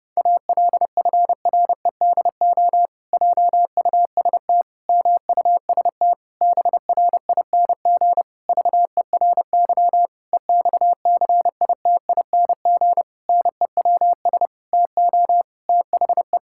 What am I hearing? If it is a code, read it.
Morse code, heard as a keyed tone, about 32 wpm, 700 Hz. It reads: ALFREDO JUST MUST BRING VERY EXCITING NEWS TO THE
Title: Pangram Challenge v2 - 30wpm